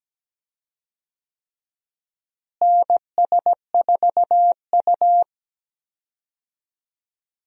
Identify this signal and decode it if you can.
Morse code, heard as a keyed tone, about 17 wpm, 700 Hz. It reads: NS4U
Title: USA Call Signs - 4 Characters - Repeated 3x - 17wpm